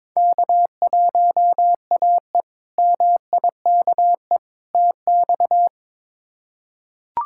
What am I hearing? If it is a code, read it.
Morse code, heard as a keyed tone, about 22 wpm, 700 Hz. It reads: K1AE MIKE TX
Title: K1USN SST QSO Practice - 22wpm